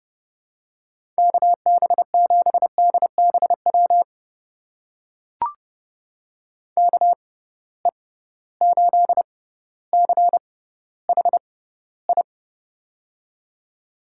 Morse code, heard as a keyed tone, about 30 wpm, spaced as for 15 wpm, 700 Hz. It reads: KB7DBW KE8CHI
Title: USA Call Signs - 6 Characters - 6x Character Spacing - 30wpm